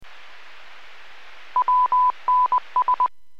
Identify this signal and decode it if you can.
Morse code, heard as a keyed tone, about 20 wpm, 1000 Hz. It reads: WNS